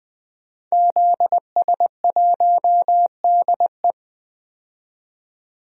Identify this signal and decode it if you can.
Morse code, heard as a keyed tone, about 20 wpm, 700 Hz. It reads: ZS1DE
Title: DXCC-100 Call Signs - 5 Characters - 20wpm